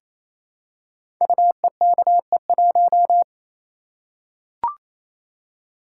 Morse code, heard as a keyed tone, about 28 wpm, 700 Hz. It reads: UEKE1
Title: Code Groups - 5 Letter:Numbers - 28wpm